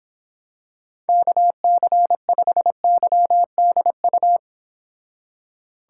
Morse code, heard as a keyed tone, about 26 wpm, 700 Hz. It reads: KC5YDU